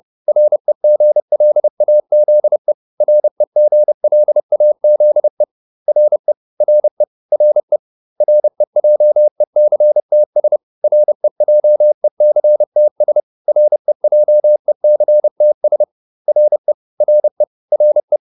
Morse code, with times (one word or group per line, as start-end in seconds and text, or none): 0.3-2.7 REGLAZE
3.0-5.4 REGLAZE
5.9-6.3 RE
6.6-7.0 RE
7.3-7.8 RE
8.2-10.6 REJECTS
10.8-13.2 REJECTS
13.5-15.8 REJECTS
16.3-16.7 RE
17.0-17.4 RE
17.7-18.2 RE